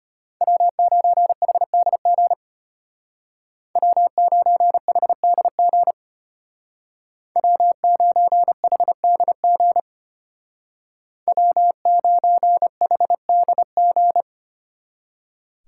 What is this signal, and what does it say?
Morse code, heard as a keyed tone, about 33 wpm, 700 Hz. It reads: W9HDG W9HDG W9HDG W9HDG